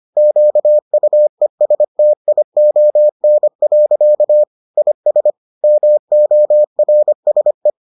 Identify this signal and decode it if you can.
Morse code, heard as a keyed tone, about 25 wpm, 600 Hz. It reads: QUESTION. IS MORSE